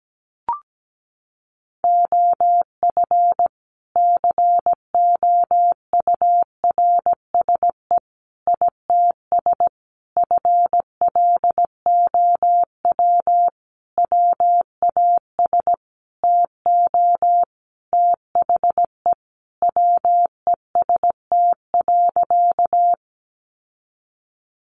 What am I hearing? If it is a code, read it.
Morse code, heard as a keyed tone, about 17 wpm, 700 Hz. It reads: OF COURSE ITS FLOW WAS TO THE WEST.
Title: Sentences from Top 1200 Words 17wpm